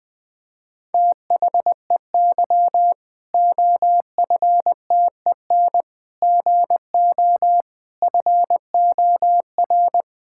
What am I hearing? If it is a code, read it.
Morse code, heard as a keyed tone, about 20 wpm, 700 Hz. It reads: THEY OFTEN GO FOR
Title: Difficult Sentences with Hints - 20wpm